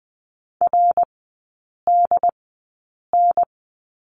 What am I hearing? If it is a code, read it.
Morse code, heard as a keyed tone, about 19 wpm, spaced as for 8 wpm, 700 Hz. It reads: RDN
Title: 087 Lesson 15e - Number 5 - Three Characters 20wpm